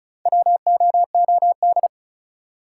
Morse code, heard as a keyed tone, about 36 wpm, 700 Hz. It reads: WOOD
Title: Four Letter Words - 35wpm